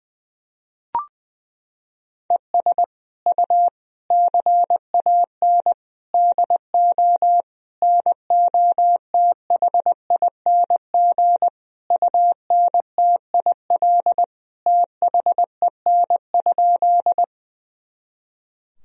Morse code, heard as a keyed tone, about 20 wpm, 700 Hz. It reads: ES U CAN DO NOTHING UNTIL THEN?
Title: Sentences from Top 500 Words and Abbreviations 20wpm